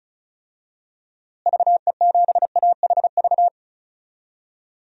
Morse code, heard as a keyed tone, about 36 wpm, 700 Hz. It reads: VE7AHV
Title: DXCC-100 Call Signs - 6 Characters - Repeated 2x - 35wpm